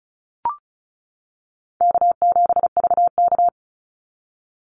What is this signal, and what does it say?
Morse code, heard as a keyed tone, about 34 wpm, 700 Hz. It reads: K7VK
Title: NA SOTA Activator Callsigns 35wpm